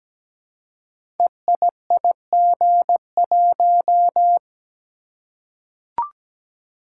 Morse code, heard as a keyed tone, about 17 wpm, 700 Hz. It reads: EIIG1
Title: Code Groups - 5 Letter:Numbers - 17wpm